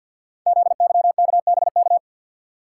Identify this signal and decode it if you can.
Morse code, heard as a keyed tone, about 48 wpm, 700 Hz. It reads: ZYKBK
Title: Code Groups - 5 Letters - 50wpm